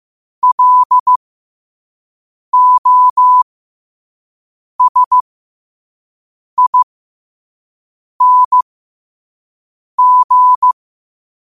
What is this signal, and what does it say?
Morse code, heard as a keyed tone, about 15 wpm, spaced as for 5 wpm, 1000 Hz. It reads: LOSING
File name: qotdfiles / 5wpm